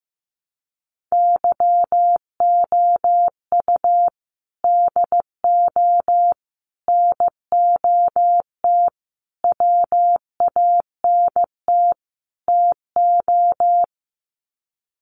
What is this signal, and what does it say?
Morse code, heard as a keyed tone, about 15 wpm, 700 Hz. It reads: YOU DO NOT WANT TO
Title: Sets of 5 Words 15wpm